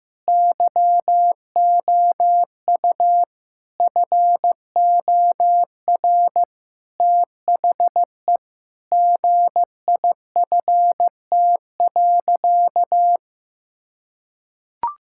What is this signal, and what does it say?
Morse code, heard as a keyed tone, about 16 wpm, 700 Hz. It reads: YOU FOR THE GIFT.
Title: Sentences from Top 1000 Words - Review Part 6 - 15wpm